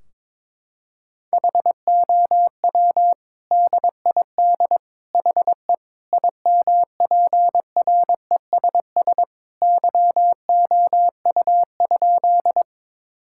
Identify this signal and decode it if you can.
Morse code, heard as a keyed tone, about 21 wpm, 700 Hz. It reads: HOW DID HE IMPRESS YOU?
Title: Sentences from Top 1600 Words 22wpm